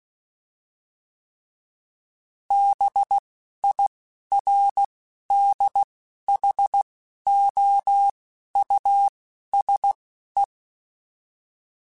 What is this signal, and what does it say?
Morse code, heard as a keyed tone, about 16 wpm, spaced as for 11 wpm, 800 Hz. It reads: BIRDHOUSE
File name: General / General-mp3s / Houses @10